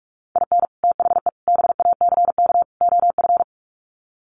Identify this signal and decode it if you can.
Morse code, heard as a keyed tone, about 44 wpm, 700 Hz. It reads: IN THE BACK OF
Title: Sets of 4 Words 45wpm